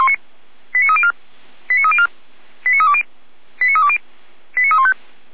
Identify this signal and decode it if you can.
Five-tone selective call (ZVEI), heard as quick runs of five tones, tones of about 70 ms. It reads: 89304 89304 89320 89320 89315